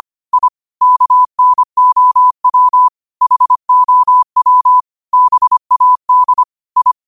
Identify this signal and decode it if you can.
Morse code, heard as a keyed tone, about 25 wpm, 1000 Hz. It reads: I KNOW HOW BAD I
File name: qotdfiles / 25wpm